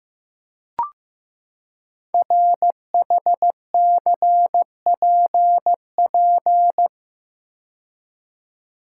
Morse code, heard as a keyed tone, about 15 wpm, 700 Hz. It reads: RHCPP
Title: Code Groups - 5 Letters - 15wpm